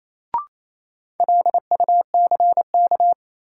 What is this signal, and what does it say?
Morse code, heard as a keyed tone, about 27 wpm, 700 Hz. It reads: LUCK
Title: Sentences from Top 1500 Words - New Word Drill Alt - 28wpm